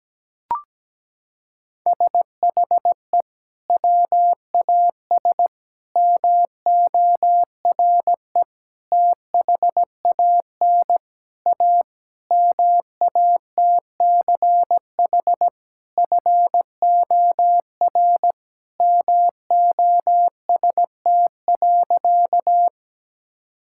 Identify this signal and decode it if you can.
Morse code, heard as a keyed tone, about 17 wpm, 700 Hz. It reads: SHE WAS MORE THAN A MATCH FOR MOST.
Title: Sentences from Top 1600 Words 17wpm